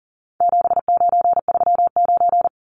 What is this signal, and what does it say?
Morse code, heard as a keyed tone, about 39 wpm, 700 Hz. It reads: 7939
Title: Numbers - 4-Digit 40wpm